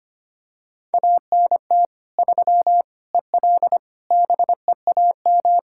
Morse code, heard as a keyed tone, about 25 wpm, 700 Hz. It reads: ANT 3 EL BEAM
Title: Practice QSO - Exchange 2 - Info - 25wpm